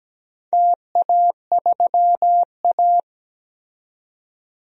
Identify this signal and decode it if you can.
Morse code, heard as a keyed tone, about 17 wpm, 700 Hz. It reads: TA3A
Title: DXCC-100 Call Signs - 4 Characters - 17wpm